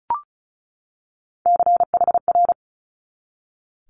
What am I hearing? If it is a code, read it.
Morse code, heard as a keyed tone, about 36 wpm, 700 Hz. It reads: CHR
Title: Multi-letter Phonemes 35wpm